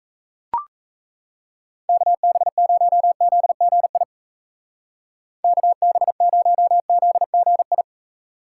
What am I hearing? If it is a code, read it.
Morse code, heard as a keyed tone, about 35 wpm, 700 Hz. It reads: KB0ZGI KB0ZGI
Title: Speed Racing - USA 2x3 Call Signs 28wpm